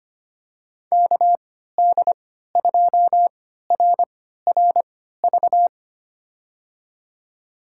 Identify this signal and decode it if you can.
Morse code, heard as a keyed tone, about 25 wpm, spaced as for 14 wpm, 700 Hz. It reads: KD2RRV
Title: USA Call Signs - 6 Characters - 3x Character Spacing - 25wpm